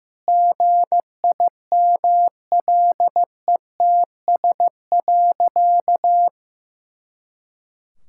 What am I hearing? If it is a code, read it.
Morse code, heard as a keyed tone, about 15 wpm, 700 Hz. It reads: GIMLETS.
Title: Pangram Challenge v2 - 15wpm